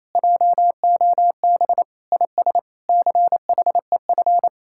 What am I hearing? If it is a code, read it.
Morse code, heard as a keyed tone, about 37 wpm, 700 Hz. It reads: JOB IS CHEF